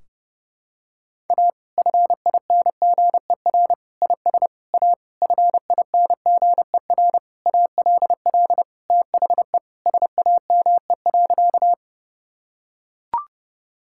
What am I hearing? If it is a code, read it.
Morse code, heard as a keyed tone, about 30 wpm, 700 Hz. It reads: A FINGER IS A FINGER ALL THE SAME.
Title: Sentences from Top 1600 Words 30wpm